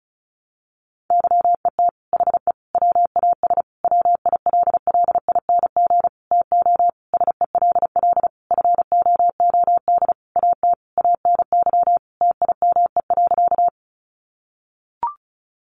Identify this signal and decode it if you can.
Morse code, heard as a keyed tone, about 34 wpm, 700 Hz. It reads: YET HE WAS WILLING TO SELL FOOD AT ANY TIME.
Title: Sentences from Top 1100 Words 35wpm